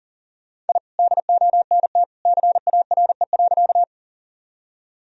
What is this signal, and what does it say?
Morse code, heard as a keyed tone, about 38 wpm, 700 Hz. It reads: I DONT CARE.
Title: Sentences from Top 1000 Words - Review Part 1 - 40wpm